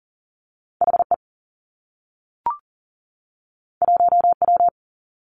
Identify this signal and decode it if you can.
Morse code, heard as a keyed tone, about 39 wpm, 700 Hz. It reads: H E 1 W
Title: Two Challenging Letter-Number Combinations 40wpm